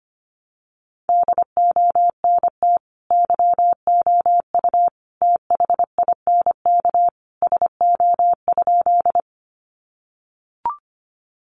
Morse code, heard as a keyed tone, about 25 wpm, 700 Hz. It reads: DONT YOU THINK SO?
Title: Sentences from Top 300 Words 25wpm